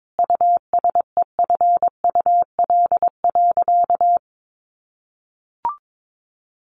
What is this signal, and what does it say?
Morse code, heard as a keyed tone, about 22 wpm, 700 Hz. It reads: USEFUL.
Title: Sentences from Top 1100 Words 22wpm